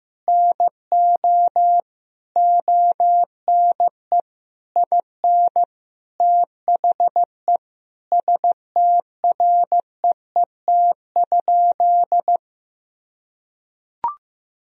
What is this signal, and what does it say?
Morse code, heard as a keyed tone, about 15 wpm, 700 Hz. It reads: NO ONE IN THE STREET?
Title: Sentences from Top 1000 Words - Review Part 3 - 15wpm